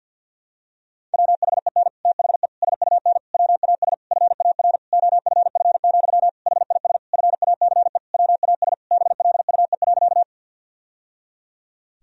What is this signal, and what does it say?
Morse code, heard as a keyed tone, about 51 wpm, 700 Hz. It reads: WHEN THE SUN WAS FAR OFF, HIS FACE WAS BLUE.
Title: Sentences from Top 1000 Words - Review Part 3 - 50wpm